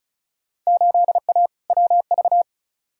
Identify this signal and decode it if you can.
Morse code, heard as a keyed tone, about 36 wpm, 700 Hz. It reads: 8A WV
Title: Field Day Exchanges Encore - 35wpm